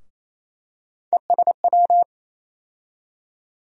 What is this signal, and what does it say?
Morse code, heard as a keyed tone, about 28 wpm, 700 Hz. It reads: ESW
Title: USA Call Sign Postfixes - 3 Letters 28wpm